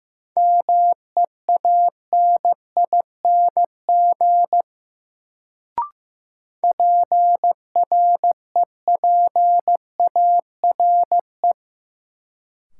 Morse code, heard as a keyed tone, about 15 wpm, 700 Hz. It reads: MEANING PREPARE
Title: Sentences from Top 900 Words - New Word Drill - 15wpm